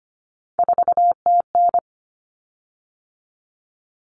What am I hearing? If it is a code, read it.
Morse code, heard as a keyed tone, about 25 wpm, 700 Hz. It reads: 4TN
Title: Common Cut Numbers - 3-Digit 25wpm